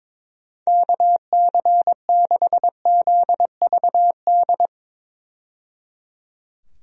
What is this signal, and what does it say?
Morse code, heard as a keyed tone, about 22 wpm, 700 Hz. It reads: KC6ZVD
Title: USA Call Signs - 6 Characters - 22wpm